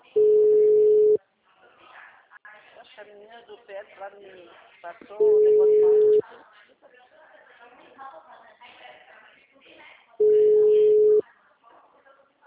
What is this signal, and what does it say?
Ringback tone (Europe, 425 Hz), with speech.
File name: out- -2024- - - .21611